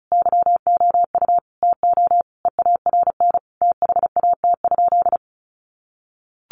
Morse code, heard as a keyed tone, about 34 wpm, 700 Hz. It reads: YOU TO EARN THAT?
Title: Sentences from Top 1500 Words 35wpm